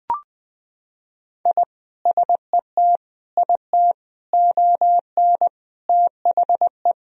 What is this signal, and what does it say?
Morse code, heard as a keyed tone, about 20 wpm, 700 Hz. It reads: I SET IT ON THE
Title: Sentences from Top 1000 Words - Review Part 2 - 20wpm